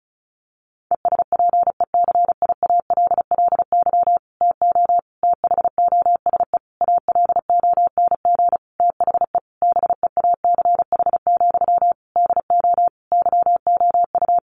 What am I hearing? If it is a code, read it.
Morse code, heard as a keyed tone, about 34 wpm, 700 Hz. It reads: ESPECIALLY TO THOSE ALONG THE BEACH, DO YOU
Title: Sentences from Top 1500 Words 35wpm